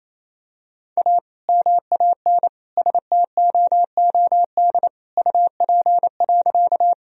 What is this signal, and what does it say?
Morse code, heard as a keyed tone, about 27 wpm, 700 Hz. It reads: A MAN STOOD UP.
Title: Sentences from Top 1000 Words - Review Part 2 - 28wpm